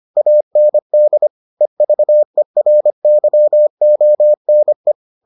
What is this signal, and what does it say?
Morse code, heard as a keyed tone, about 25 wpm, 600 Hz. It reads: AND EVERYONE